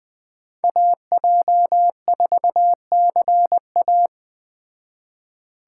Morse code, heard as a keyed tone, about 20 wpm, 700 Hz. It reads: AJ4CA